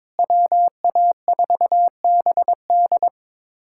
Morse code, heard as a keyed tone, about 22 wpm, 700 Hz. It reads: WA4BD
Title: USA Callsign DE Callsign - 5 Characters - 22wpm